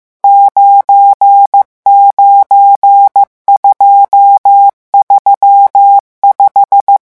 Morse code, heard as a keyed tone, about 15 wpm, 800 Hz. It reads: 99235
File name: morse mp3 / revision11 chiffres vitesse 14